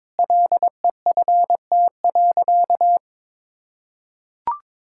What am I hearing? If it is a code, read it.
Morse code, heard as a keyed tone, about 22 wpm, 700 Hz. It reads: LEFT.
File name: practice / 22wpm / Sentences with 4-Word Phrases - 22wpm